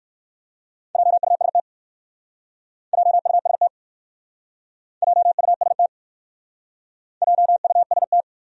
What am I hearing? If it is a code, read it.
Morse code, heard as a keyed tone, about 60 wpm, 700 Hz. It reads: JUST JUST JUST JUST